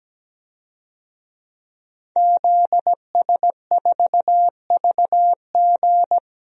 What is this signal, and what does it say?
Morse code, heard as a keyed tone, about 17 wpm, 700 Hz. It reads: ZS4VG
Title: DXCC-100 Call Signs - 5 Characters - Repeated 3x - 17wpm